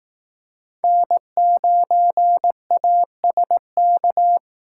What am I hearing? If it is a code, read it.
Morse code, heard as a keyed tone, about 18 wpm, 700 Hz. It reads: N9ASK
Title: Speed Racing - USA Call Signs 15wpm